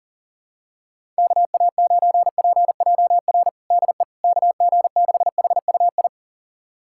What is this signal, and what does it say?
Morse code, heard as a keyed tone, about 37 wpm, 700 Hz. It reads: KA9PJR DE KG6HUI
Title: USA Callsign DE Callsign 40wpm